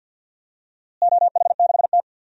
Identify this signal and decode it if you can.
Morse code, heard as a keyed tone, about 50 wpm, 700 Hz. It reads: OH6T
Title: DXCC-100 Call Signs - 4 Characters - Repeated 3x - 50wpm